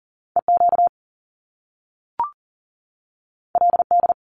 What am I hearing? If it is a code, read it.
Morse code, heard as a keyed tone, about 41 wpm, 700 Hz. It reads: EQ LD
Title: Top Two Letter Combinations in Words 40wpm